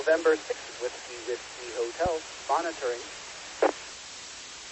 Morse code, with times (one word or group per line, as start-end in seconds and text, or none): none